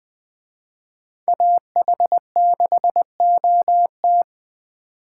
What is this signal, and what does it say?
Morse code, heard as a keyed tone, about 21 wpm, 700 Hz. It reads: AH6OT